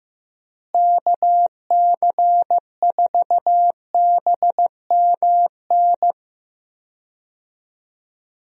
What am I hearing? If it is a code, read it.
Morse code, heard as a keyed tone, about 15 wpm, 700 Hz. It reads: KC4BMN
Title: USA Call Signs - 6 Characters - Repeated 3x - 15wpm